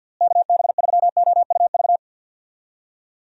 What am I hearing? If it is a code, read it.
Morse code, heard as a keyed tone, about 46 wpm, 700 Hz. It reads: KB3OUV
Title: USA Call Signs - 6 Characters - Repeated 2x - 50wpm